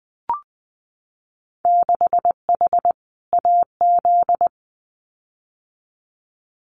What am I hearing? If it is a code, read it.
Morse code, heard as a keyed tone, about 20 wpm, 700 Hz. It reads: 6H AZ
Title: Winter Field Day Exchanges Encore 20wpm